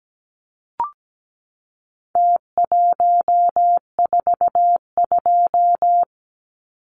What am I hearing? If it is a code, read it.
Morse code, heard as a keyed tone, about 17 wpm, 700 Hz. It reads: T142
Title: Numbers - Contest Style - 4-Digit 17wpm